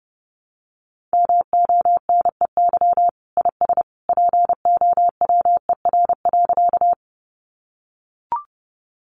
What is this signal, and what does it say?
Morse code, heard as a keyed tone, about 30 wpm, 700 Hz. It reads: MONEY IS POWER.
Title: Sentences from Top 500 Words 30wpm